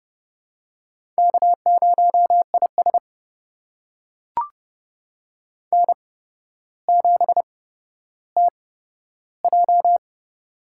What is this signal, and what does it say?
Morse code, heard as a keyed tone, about 32 wpm, spaced as for 13 wpm, 700 Hz. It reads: K0IS N7TJ
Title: USA Call Signs - 4 Characters - 8x Character Spacing - 30wpm